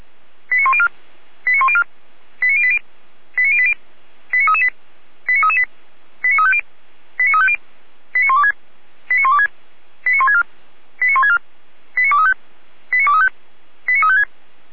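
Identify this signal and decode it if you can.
Five-tone selective call (ZVEI), heard as quick runs of five tones, tones of about 70 ms. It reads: 89104 89104 89080 89080 89338 89338 89350 89350 89126 89126 89164 89164 89235 89235 89356
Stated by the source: ZVEI five-tone selective call received by SDR.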